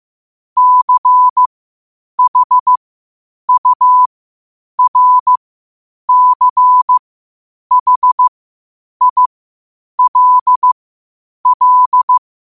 Morse code, heard as a keyed tone, about 15 wpm, spaced as for 8 wpm, 1000 Hz. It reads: CHURCHILL
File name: qotdfiles / 8wpm